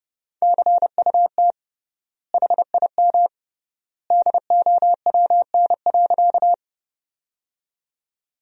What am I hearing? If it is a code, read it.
Morse code, heard as a keyed tone, about 30 wpm, 700 Hz. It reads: CUT HIM DOWN.
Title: 3x Word-Spacing - Sentences from Top 300 Words - 30wpm